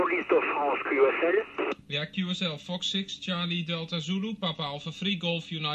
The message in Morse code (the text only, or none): none